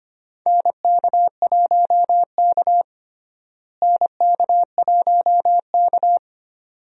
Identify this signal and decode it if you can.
Morse code, heard as a keyed tone, about 25 wpm, 700 Hz. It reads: NK1K NK1K
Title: USA Call Signs - 4 Characters - Repeated 2x - 25wpm